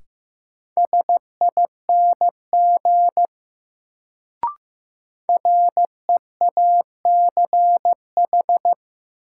Morse code, heard as a keyed tone, about 15 wpm, 700 Hz. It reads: SING REACH
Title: Sentences from Top 400 Words - New Word Drill - 15wpm